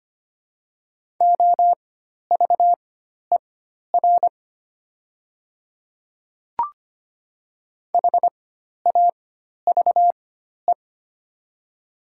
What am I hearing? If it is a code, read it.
Morse code, heard as a keyed tone, about 25 wpm, spaced as for 12 wpm, 700 Hz. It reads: OVER HAVE
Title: 4x Character Spacing - Top 100 Words - 25wpm